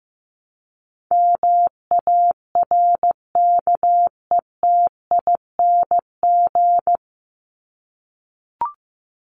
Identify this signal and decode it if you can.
Morse code, heard as a keyed tone, about 15 wpm, 700 Hz. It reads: MARKETING
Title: Top 751-1000 Words 15wpm